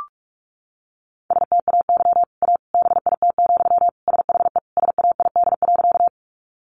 Morse code, heard as a keyed tone, about 44 wpm, 700 Hz. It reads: STAY A BIT, SHE SAID.
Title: Sentences from Top 900 Words 45wpm